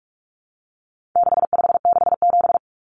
Morse code, heard as a keyed tone, about 44 wpm, 700 Hz. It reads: 6567
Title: Numbers - 4-Digit 45wpm